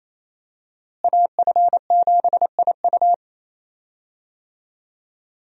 Morse code, heard as a keyed tone, about 28 wpm, 700 Hz. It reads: AF7IU
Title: USA Call Signs - 5 Characters - Repeated 2x - 28wpm